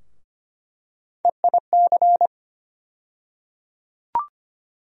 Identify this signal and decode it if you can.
Morse code, heard as a keyed tone, about 25 wpm, 700 Hz. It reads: EIC
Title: USA Call Sign Postfixes - 3 Letters 25wpm